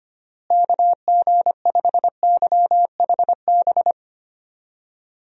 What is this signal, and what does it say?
Morse code, heard as a keyed tone, about 25 wpm, 700 Hz. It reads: KG5YHB
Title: USA Call Signs - 6 Characters - Repeated 2x - 25wpm